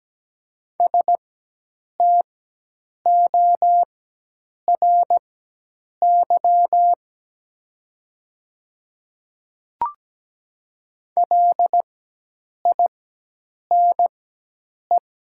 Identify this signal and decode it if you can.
Morse code, heard as a keyed tone, about 17 wpm, spaced as for 8 wpm, 700 Hz. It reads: STORY LINE
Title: 4x Character Spacing - Top 500 Words - 17wpm